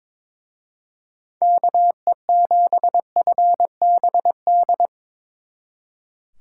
Morse code, heard as a keyed tone, about 25 wpm, 700 Hz. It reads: KE7FBD
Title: Speed Racing - USA 2x3 Call Signs 22wpm